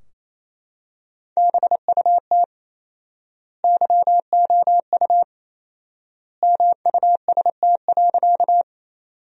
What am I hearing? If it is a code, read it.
Morse code, heard as a keyed tone, about 28 wpm, 700 Hz. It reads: BUT YOU MUST.
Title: 4x Word-Spacing - Sentences from Top 200 Words - 28wpm